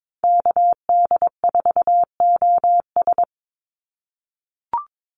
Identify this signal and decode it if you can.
Morse code, heard as a keyed tone, about 22 wpm, 700 Hz. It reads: KD4OS
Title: Call Signs - 22wpm